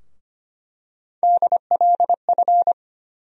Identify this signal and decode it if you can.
Morse code, heard as a keyed tone, about 25 wpm, 700 Hz. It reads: DLF
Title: USA Call Sign Postfixes - 3 Letters 25wpm